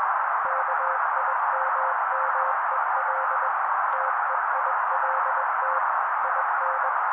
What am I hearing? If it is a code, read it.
Morse code, heard as a keyed tone, about 21 wpm, 530 Hz. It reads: KIMMEL TEILT IN
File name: daily / 20wpm noise